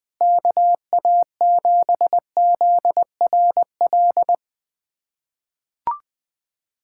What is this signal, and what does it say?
Morse code, heard as a keyed tone, about 20 wpm, 700 Hz. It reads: KA7ZRL
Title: USA Callsign DE Callsign 20wpm